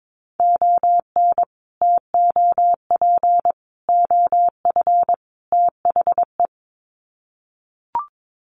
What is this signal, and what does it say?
Morse code, heard as a keyed tone, about 22 wpm, 700 Hz. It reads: ON TOP OF THE
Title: Sets of 4 Words 22wpm